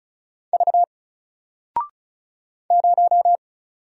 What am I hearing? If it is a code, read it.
Morse code, heard as a keyed tone, about 36 wpm, 700 Hz. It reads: V 0
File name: Single-Letter-Number-35wpm